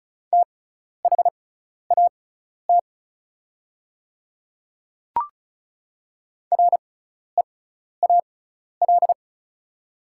Morse code, heard as a keyed tone, about 36 wpm, spaced as for 12 wpm, 700 Hz. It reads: THAT REAL
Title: 6x Character Spacing - Top 500 Words - 35wpm